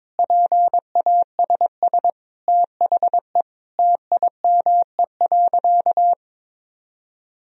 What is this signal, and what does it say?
Morse code, heard as a keyed tone, about 21 wpm, 700 Hz. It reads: PASS THE TIME.
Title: Sentences from Top 1000 Words - Review Part 4 - 22wpm